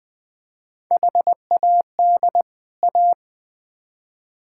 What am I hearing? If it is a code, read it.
Morse code, heard as a keyed tone, about 20 wpm, 700 Hz. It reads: HAD A
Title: Build A Phrase - 4-Word Phrases - 20wpm